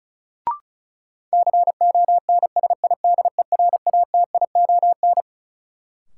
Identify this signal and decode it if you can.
Morse code, heard as a keyed tone, about 35 wpm, 700 Hz. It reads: CONSIDERATION
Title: Sentences from Top 1500 Words - New Word Drill - 35wpm